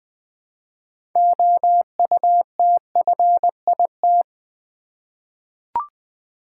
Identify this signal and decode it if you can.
Morse code, heard as a keyed tone, about 20 wpm, 700 Hz. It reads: OUTFIT
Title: 6 Letter Words - First 3 Letters Given - 20wpm